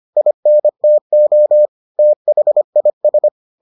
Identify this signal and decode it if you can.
Morse code, heard as a keyed tone, about 25 wpm, 600 Hz. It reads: INTO THIS